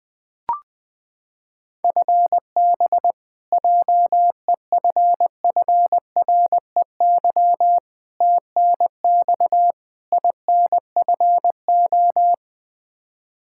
Morse code, heard as a keyed tone, about 20 wpm, 700 Hz. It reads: FB JEFFREY TNX INFO
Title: K1USN SST QSO Practice 20wpm